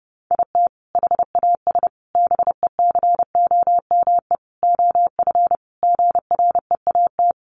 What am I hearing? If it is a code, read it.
Morse code, heard as a keyed tone, about 30 wpm, 700 Hz. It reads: IT HAS BECOME OF GREAT
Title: Sentences from Top 1200 Words 30wpm